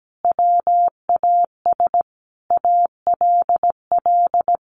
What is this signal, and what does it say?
Morse code, heard as a keyed tone, about 17 wpm, 700 Hz. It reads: WAS ALL
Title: Sentences from Top 700 Words 17wpm